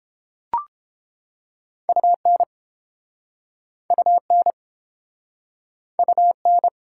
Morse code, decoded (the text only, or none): UN UN UN